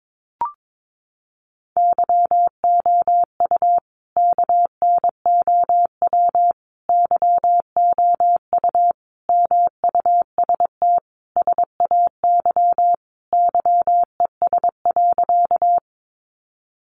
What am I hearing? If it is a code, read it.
Morse code, heard as a keyed tone, about 22 wpm, 700 Hz. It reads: YOU KNOW YOU MUST SAY YES.
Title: Sentences from Top 500 Words 22wpm